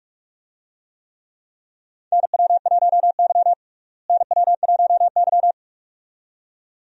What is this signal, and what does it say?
Morse code, heard as a keyed tone, about 46 wpm, 700 Hz. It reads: NW1Y NW1Y
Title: USA Call Signs - 4 Characters - Repeated 2x - 45wpm